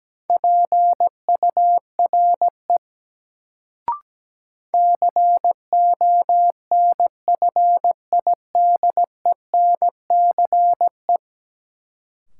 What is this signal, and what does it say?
Morse code, heard as a keyed tone, about 17 wpm, 700 Hz. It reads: PURE CONFIDENCE
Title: Sentences from Top 1500 Words - New Word Drill - 17wpm